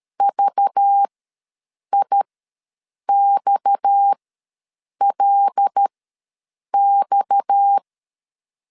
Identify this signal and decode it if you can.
Morse code, heard as a keyed tone, about 13 wpm, spaced as for 7 wpm, 800 Hz. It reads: VIXLX
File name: LEZIONI / 6 PKX0 / 6 PKX ws 12wpm 3106